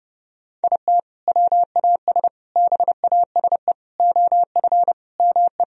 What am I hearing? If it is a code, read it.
Morse code, heard as a keyed tone, about 30 wpm, 700 Hz. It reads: IT WAS BASE OF ME
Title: Sentences from Top 1000 Words - Review Part 1 - 30wpm